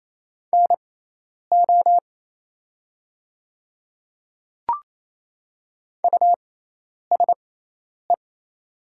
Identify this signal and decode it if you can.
Morse code, heard as a keyed tone, about 28 wpm, spaced as for 10 wpm, 700 Hz. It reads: NO USE
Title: 6x Character Spacing - Top 100 Words - 28wpm